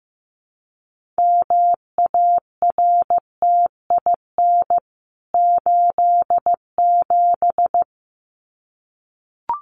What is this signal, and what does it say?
Morse code, heard as a keyed tone, about 15 wpm, 700 Hz. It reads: MARTIN 87
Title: CWT Exchange 15wpm